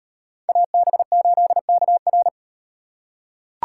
Morse code, heard as a keyed tone, about 36 wpm, 700 Hz. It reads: AB8KR